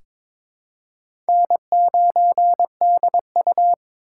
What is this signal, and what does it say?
Morse code, heard as a keyed tone, about 22 wpm, 700 Hz. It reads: N9DU
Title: USA Call Signs - 4 Characters - 6x Character Spacing - 22wpm